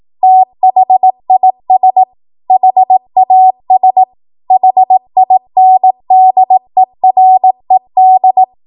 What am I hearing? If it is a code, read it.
Morse code, heard as a keyed tone, about 18 wpm, 750 Hz. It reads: THIS HAS HINDERED
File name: Morse / 18WPM